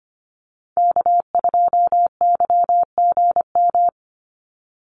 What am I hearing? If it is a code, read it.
Morse code, heard as a keyed tone, about 25 wpm, 700 Hz. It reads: K2YGM
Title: Call Signs - 25wpm